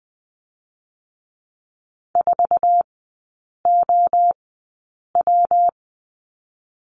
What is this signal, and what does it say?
Morse code, heard as a keyed tone, about 20 wpm, 700 Hz. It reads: 4 O W
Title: 114 Lesson 19e - Letter w - Three Characters 20wpm